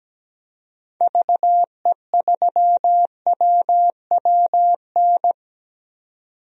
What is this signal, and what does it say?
Morse code, heard as a keyed tone, about 17 wpm, 700 Hz. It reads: VE3WWN
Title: DXCC-100 Call Signs - 6 Characters - 17wpm